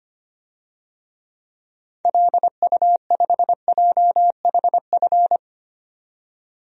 Morse code, heard as a keyed tone, about 25 wpm, 700 Hz. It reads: LU5JHF
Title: DXCC-100 Call Signs - 6 Characters - 25wpm